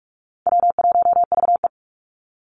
Morse code, heard as a keyed tone, about 44 wpm, 700 Hz. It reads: W1VE
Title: Call Signs - 45wpm